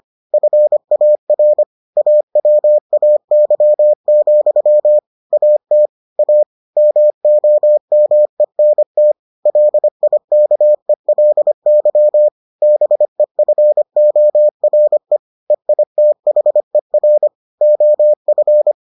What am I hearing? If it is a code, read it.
Morse code, heard as a keyed tone, about 25 wpm, 600 Hz. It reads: FAR AWAY, AT A MOMENT LIKELY BEFORE EITHER OF